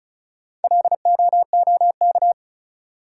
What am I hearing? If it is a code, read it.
Morse code, heard as a keyed tone, about 34 wpm, 700 Hz. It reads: LOOK
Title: Sentences from Top 100 Words - New Word Drill - 35wpm